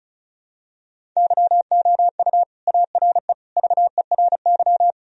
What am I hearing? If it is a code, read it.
Morse code, heard as a keyed tone, about 35 wpm, 700 Hz. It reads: YOU ARE VERY
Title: Sentences from Top 1000 Words - Review Part 1 - 35wpm